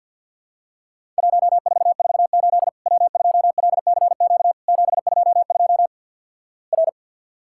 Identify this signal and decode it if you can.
Morse code, heard as a keyed tone, about 50 wpm, 700 Hz. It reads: 1448 W2LCQ 722 R
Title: Worked All Europe DX Contest - QTC Exchange - 50wpm